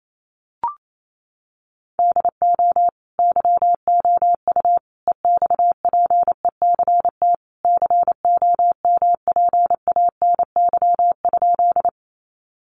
Morse code, heard as a keyed tone, about 28 wpm, 700 Hz. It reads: DO YOU EXPECT COMPANY?
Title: Sentences from Top 1100 Words 28wpm